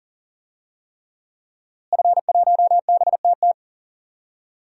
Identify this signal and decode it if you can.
Morse code, heard as a keyed tone, about 40 wpm, 700 Hz. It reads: F1BTT
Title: DXCC-100 Call Signs - 5 Characters - Repeated 3x - 40wpm